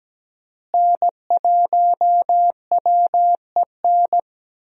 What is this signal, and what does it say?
Morse code, heard as a keyed tone, about 17 wpm, 700 Hz. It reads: N1WEN
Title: USA Call Signs - 5 Characters - Repeated 2x - 17wpm